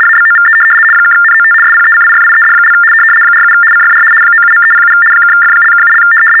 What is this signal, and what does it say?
RTTY, about 75 baud. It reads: ABOVE (NOT SPECIFIED) 47 CFR 97.307 (F)(7)
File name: teletype-RTTY-75